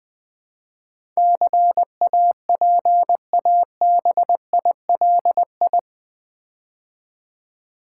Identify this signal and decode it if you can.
Morse code, heard as a keyed tone, about 20 wpm, 700 Hz. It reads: CAPABILI
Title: Build A Word - 4 Letters At A Time - 10-Letter Words - 20wpm